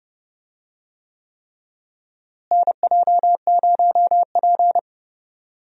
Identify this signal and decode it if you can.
Morse code, heard as a keyed tone, about 30 wpm, 700 Hz. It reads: NJ0P
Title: USA Call Signs - 4 Characters - Repeated 3x - 30wpm